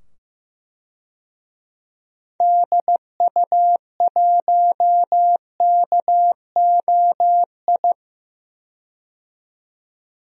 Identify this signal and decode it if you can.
Morse code, heard as a keyed tone, about 15 wpm, 700 Hz. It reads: DU1KOI
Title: DXCC-100 Call Signs - 6 Characters - 15wpm